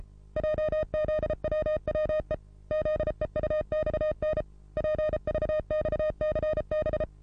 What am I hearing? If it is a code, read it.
Morse code, heard as a keyed tone, about 30 wpm, 620 Hz. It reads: JZWWE ZEUXN PVXCB
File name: repos / Morsekurs / DJ4UF Morsen27